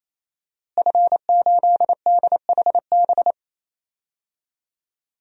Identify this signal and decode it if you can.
Morse code, heard as a keyed tone, about 28 wpm, 700 Hz. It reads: F8DHB
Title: DXCC-100 Call Signs - 5 Characters - Repeated 3x - 28wpm